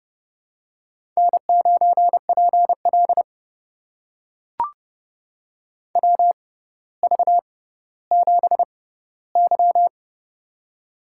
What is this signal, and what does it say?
Morse code, heard as a keyed tone, about 30 wpm, spaced as for 15 wpm, 700 Hz. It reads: N9PL WV7Y
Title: USA Call Signs - 4 Characters - 6x Character Spacing - 30wpm